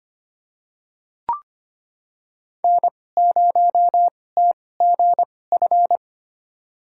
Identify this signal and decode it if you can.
Morse code, heard as a keyed tone, about 25 wpm, spaced as for 18 wpm, 700 Hz. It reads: N0TGF
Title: USA Call Signs - 5 Characters - 2x Character Spacing - 25wpm